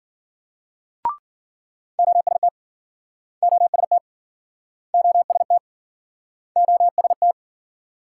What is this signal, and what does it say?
Morse code, heard as a keyed tone, about 52 wpm, 700 Hz. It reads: OST OST OST OST